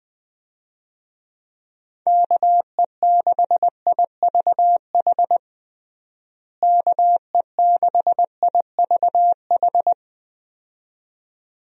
Morse code, heard as a keyed tone, about 20 wpm, 700 Hz. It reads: KE6IVH KE6IVH
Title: USA Call Signs - 6 Characters - Repeated 2x - 20wpm